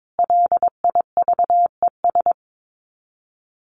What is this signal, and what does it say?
Morse code, heard as a keyed tone, about 22 wpm, 700 Hz. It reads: LIVES
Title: Sets of 5 Words Encore 22wpm